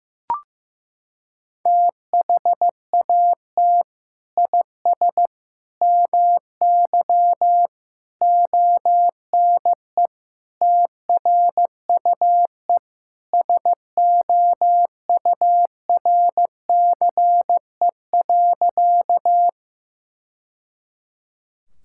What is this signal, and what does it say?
Morse code, heard as a keyed tone, about 15 wpm, 700 Hz. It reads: THAT IS MY ONE TRUE SOURCE.
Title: Sentences from Top 1000 Words - Review Part 5 - 15wpm